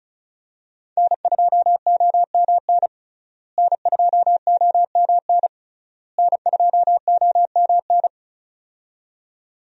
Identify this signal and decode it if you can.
Morse code, heard as a keyed tone, about 36 wpm, 700 Hz. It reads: N2OMN N2OMN N2OMN
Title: USA Call Signs - 5 Characters - Repeated 3x - 35wpm